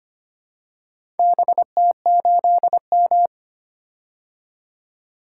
Morse code, heard as a keyed tone, about 25 wpm, 700 Hz. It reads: BT8M
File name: practice / 25wpm / Four Letter-Number Combinations - 25wpm